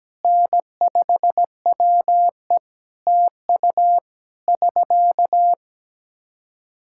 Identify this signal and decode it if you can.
Morse code, heard as a keyed tone, about 17 wpm, 700 Hz. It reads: N5WE TU <SK>
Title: Practice QSO - Exchange 4 - Ending - 17wpm